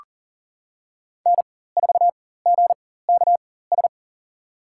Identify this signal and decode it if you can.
Morse code, heard as a keyed tone, about 40 wpm, spaced as for 19 wpm, 700 Hz. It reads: N4GKS
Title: USA Call Signs - 5 Characters - 4x Character Spacing - 40wpm